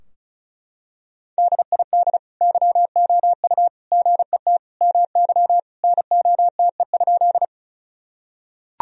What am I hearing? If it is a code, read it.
Morse code, heard as a keyed tone, about 36 wpm, 700 Hz. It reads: DID YOU GET MY NOTE?
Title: Sentences from Top 1000 Words - Review Part 2 - 35wpm